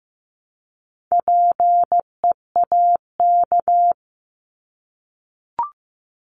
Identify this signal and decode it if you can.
Morse code, heard as a keyed tone, about 15 wpm, 700 Hz. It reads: PEAK
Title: Top 1751-2000 Words 15wpm